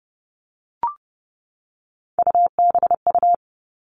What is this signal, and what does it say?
Morse code, heard as a keyed tone, about 30 wpm, 700 Hz. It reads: UBU
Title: All Cut Numbers - 3-Digit 30wpm